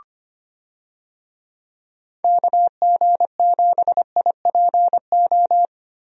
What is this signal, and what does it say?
Morse code, heard as a keyed tone, about 25 wpm, 700 Hz. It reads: KG7IPO
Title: USA Call Signs - 6 Characters - Repeated 2x - 25wpm